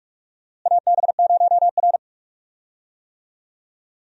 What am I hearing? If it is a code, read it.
Morse code, heard as a keyed tone, about 41 wpm, 700 Hz. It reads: AB0R
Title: USA Call Signs - 4 Characters - Repeated 2x - 45wpm